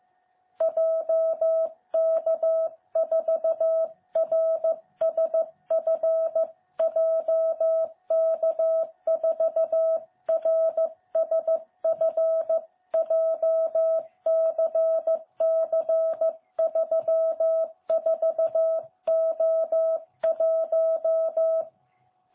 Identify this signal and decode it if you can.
Morse code, heard as a keyed tone, about 14 wpm, 660 Hz. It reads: JK4RSF JK4RSF JCC34O1
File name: jh8cbh / 2021cw